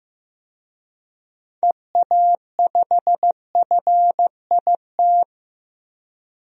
Morse code, heard as a keyed tone, about 15 wpm, 700 Hz. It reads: EA5FIT
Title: DXCC-100 Call Signs - 6 Characters - Repeated 3x - 15wpm